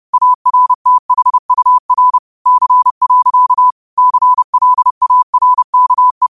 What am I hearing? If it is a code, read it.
Morse code, heard as a keyed tone, about 30 wpm, 1000 Hz. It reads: ARTHUR C. CLARKE
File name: qotdfiles / 30wpm